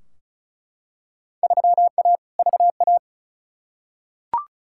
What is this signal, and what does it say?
Morse code, heard as a keyed tone, about 36 wpm, 700 Hz. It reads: 3A VA
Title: Field Day Exchanges - 35wpm